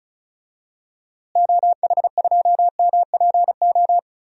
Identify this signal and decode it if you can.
Morse code, heard as a keyed tone, about 34 wpm, 700 Hz. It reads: OH2MPO
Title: DXCC-100 Call Signs - 6 Characters - Repeated 3x - 35wpm